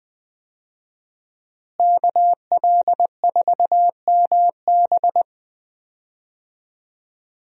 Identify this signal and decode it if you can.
Morse code, heard as a keyed tone, about 20 wpm, 700 Hz. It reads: KL4MB
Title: USA Call Signs - 5 Characters - Repeated 2x - 20wpm